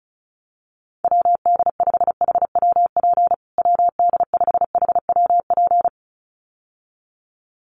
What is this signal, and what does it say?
Morse code, heard as a keyed tone, about 34 wpm, 700 Hz. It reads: WD5HWP WD5HWP
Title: USA Call Signs Repeated 2x 35wpm